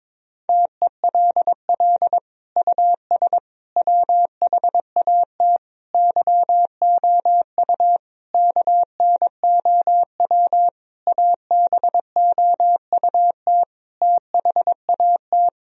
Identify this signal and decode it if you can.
Morse code, heard as a keyed tone, about 21 wpm, 700 Hz. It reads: TELL US WHAT YOU KNOW ABOUT THAT
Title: Sentences from Top 1000 Words - Review Part 4 - 22wpm